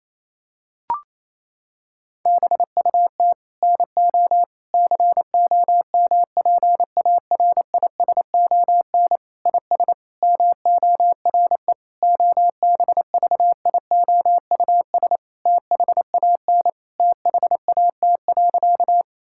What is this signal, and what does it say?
Morse code, heard as a keyed tone, about 28 wpm, 700 Hz. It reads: BUT NO COMPARISON IS MORE OBVIOUS THAN THAT.
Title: Sentences from Top 1600 Words 28wpm